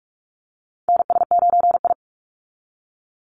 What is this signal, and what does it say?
Morse code, heard as a keyed tone, about 44 wpm, 700 Hz. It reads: NS9I
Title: Call Signs Repeated 2x - 45wpm